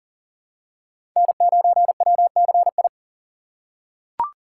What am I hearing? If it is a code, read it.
Morse code, heard as a keyed tone, about 40 wpm, 700 Hz. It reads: N9WCI
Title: USA Call Signs - 5 Characters - 2x Character Spacing - 40wpm